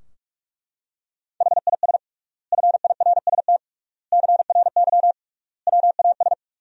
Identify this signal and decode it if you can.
Morse code, heard as a keyed tone, about 41 wpm, 700 Hz. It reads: HIS FIRST CRY WAS
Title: 3x Word-Spacing - Sentences from Top 500 Words - 45wpm